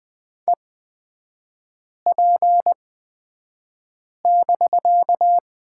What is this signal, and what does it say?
Morse code, heard as a keyed tone, about 20 wpm, spaced as for 5 wpm, 700 Hz. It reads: EP<BK>
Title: 250 Lesson 40e - BK - Three Characters 20wpm